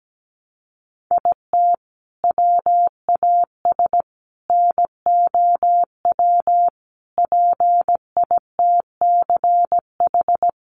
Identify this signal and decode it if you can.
Morse code, heard as a keyed tone, about 17 wpm, 700 Hz. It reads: IT WAS NOW PITCH
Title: Sentences from Top 1300 Words 17wpm